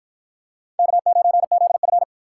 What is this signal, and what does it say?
Morse code, heard as a keyed tone, about 53 wpm, 700 Hz. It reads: K9ZF